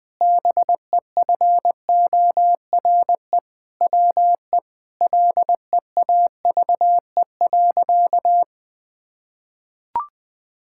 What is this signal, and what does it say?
Morse code, heard as a keyed tone, about 20 wpm, 700 Hz. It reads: BEFORE WE LEAVE.
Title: Sentences from Top 1000 Words - Review Part 5 - 20wpm